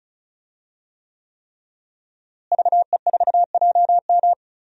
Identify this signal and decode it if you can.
Morse code, heard as a keyed tone, about 36 wpm, 700 Hz. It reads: VE4JM
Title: DXCC-100 Call Signs - 5 Characters - Repeated 3x - 35wpm